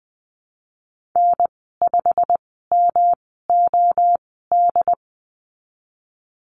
Morse code, heard as a keyed tone, about 20 wpm, spaced as for 14 wpm, 700 Hz. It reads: N5MOD
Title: 122 Lesson 20g - Number 3 - Call Signs 20wpm